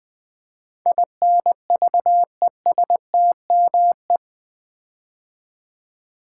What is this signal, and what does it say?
Morse code, heard as a keyed tone, about 20 wpm, 700 Hz. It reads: INVESTME
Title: Build A Word - 4 Letters At A Time - 10-Letter Words - 20wpm